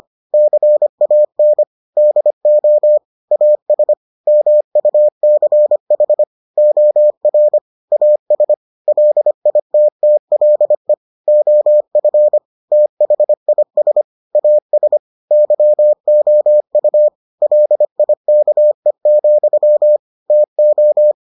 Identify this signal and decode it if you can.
Morse code, heard as a keyed tone, about 25 wpm, 600 Hz. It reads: CAN DO AS MUCH OR AS LITTLE OF THIS AS YOU LIKE, TO